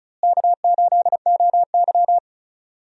Morse code, heard as a keyed tone, about 36 wpm, 700 Hz. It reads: K8OY
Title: USA Call Signs - 4 Characters - Repeated 3x - 35wpm